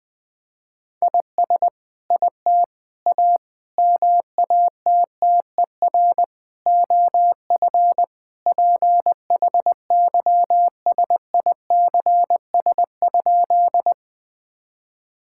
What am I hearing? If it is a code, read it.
Morse code, heard as a keyed tone, about 21 wpm, 700 Hz. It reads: IS IT A MATTER OF PHYSICS?
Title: Sentences from Top 1000 Words - Review Part 5 - 20wpm